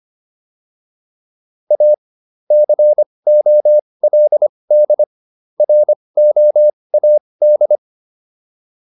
Morse code, heard as a keyed tone, about 25 wpm, spaced as for 20 wpm, 600 Hz. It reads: A COLD ROAD
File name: PR303 20